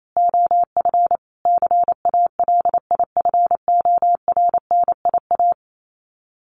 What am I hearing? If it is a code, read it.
Morse code, heard as a keyed tone, about 28 wpm, 700 Hz. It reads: OF CALIFORNIA
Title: Sets of 5 Words Encore2 28wpm